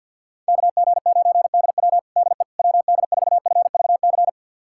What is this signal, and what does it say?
Morse code, heard as a keyed tone, about 51 wpm, 700 Hz. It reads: KG9DW DE WD4FVC
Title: USA Callsign DE Callsign 50wpm